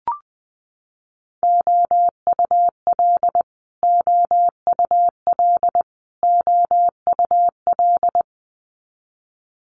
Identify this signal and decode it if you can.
Morse code, heard as a keyed tone, about 20 wpm, 700 Hz. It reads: OUL OUL OUL
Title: Multi-letter Phonemes Repeated 3x 20wpm